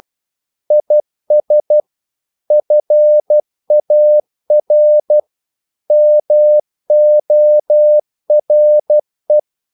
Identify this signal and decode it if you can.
Morse code, heard as a keyed tone, about 12 wpm, 600 Hz. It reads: IS FAR MORE
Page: English quotations in CW (telegraphy - morse)